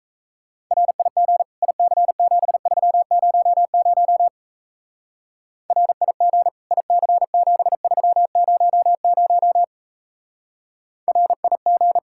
Morse code, heard as a keyed tone, about 33 wpm, 700 Hz. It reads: RIG IC7300 RIG IC7300 RIG
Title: Speed-Racing Common QSO Phrases Encore - 28wpm